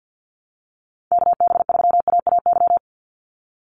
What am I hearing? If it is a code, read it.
Morse code, heard as a keyed tone, about 50 wpm, 700 Hz. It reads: KB3AAY
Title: Call Signs - 50wpm